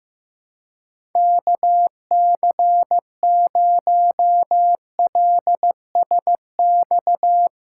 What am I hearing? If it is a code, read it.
Morse code, heard as a keyed tone, about 15 wpm, 700 Hz. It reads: KC0LSX
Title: USA Call Signs - 2x Character Spacing - 15wpm